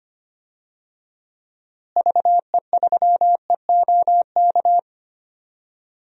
Morse code, heard as a keyed tone, about 25 wpm, 700 Hz. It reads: VE3EOK
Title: DXCC-100 Call Signs - 6 Characters - Repeated 3x - 25wpm